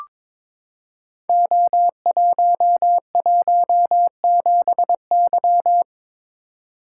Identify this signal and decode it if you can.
Morse code, heard as a keyed tone, about 22 wpm, 700 Hz. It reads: O117Y
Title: Code Groups - 5 Letter:Numbers - 22wpm